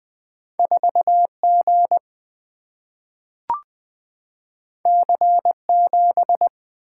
Morse code, heard as a keyed tone, about 20 wpm, 700 Hz. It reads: 4G C7
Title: Single-Letter Single-Number 20wpm